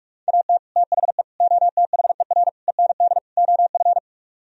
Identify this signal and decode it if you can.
Morse code, heard as a keyed tone, about 49 wpm, 700 Hz. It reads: AT THE OTHER END OF